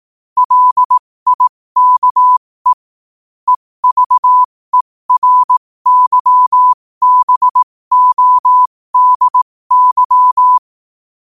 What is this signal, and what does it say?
Morse code, heard as a keyed tone, about 18 wpm, spaced as for 15 wpm, 1000 Hz. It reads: LIKE EVERYBODY
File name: qotdfiles / 15wpm